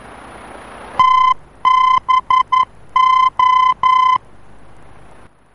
Morse code, heard as a keyed tone, about 11 wpm, 1030 Hz. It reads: TBO
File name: france / mx / TBO-F-VOR